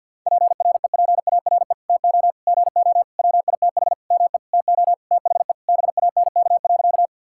Morse code, heard as a keyed tone, about 50 wpm, 700 Hz. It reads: PREPARE TO GO WITH ME TO THE BANK.
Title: Sentences from Top 1000 Words - Review Part 6 - 50wpm